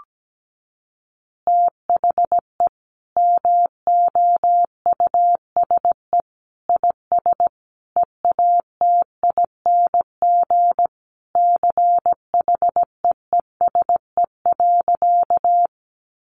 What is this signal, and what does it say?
Morse code, heard as a keyed tone, about 17 wpm, 700 Hz. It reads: THE MOUSE IS EATING CHEESE.
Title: Easy Phrases 17wpm